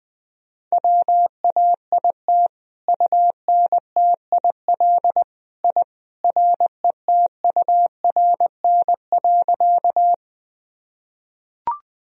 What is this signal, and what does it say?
Morse code, heard as a keyed tone, about 20 wpm, 700 Hz. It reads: WAIT UNTIL I RETURN.
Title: Sentences from Top 1000 Words - Review Part 1 - 20wpm